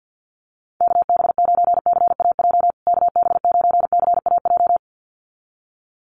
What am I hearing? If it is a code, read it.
Morse code, heard as a keyed tone, about 50 wpm, 700 Hz. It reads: KB9CAJ KB9CAJ
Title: USA Call Signs Repeated 2x 50wpm